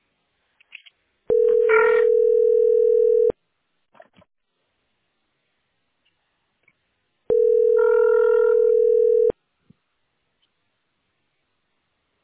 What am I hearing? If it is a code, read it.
Ringback tone (North America), 440 + 480 Hz.
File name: 86e6e8c2-daea-419c-a684